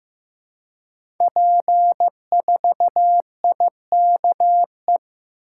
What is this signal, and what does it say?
Morse code, heard as a keyed tone, about 15 wpm, 700 Hz. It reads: P4IKE
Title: Code Groups - 5 Letter:Numbers - 15wpm